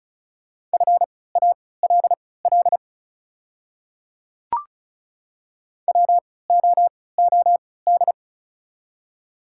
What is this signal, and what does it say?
Morse code, heard as a keyed tone, about 36 wpm, spaced as for 20 wpm, 700 Hz. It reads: FALL WOOD
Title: 3x Character Spacing - Top 500 Words - 35wpm